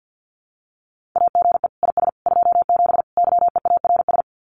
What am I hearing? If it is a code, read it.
Morse code, heard as a keyed tone, about 50 wpm, 700 Hz. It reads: AGE IS 27 YEARS
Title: Common QSO Phrases Encore 50wpm